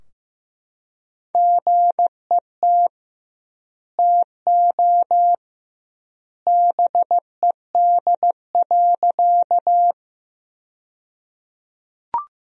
Morse code, heard as a keyed tone, about 16 wpm, 700 Hz. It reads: GET TO BED.
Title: 2x Word-Spacing - Sentences from Top 500 Words - 15wpm